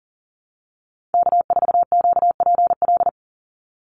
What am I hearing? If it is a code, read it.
Morse code, heard as a keyed tone, about 40 wpm, 700 Hz. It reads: K4QPL
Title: Call Signs Repeated 2x - 40wpm